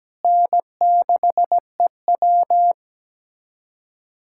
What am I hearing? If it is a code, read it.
Morse code, heard as a keyed tone, about 17 wpm, 700 Hz. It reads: N6EW
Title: USA Callsign DE Callsign - 4 Characters - 17wpm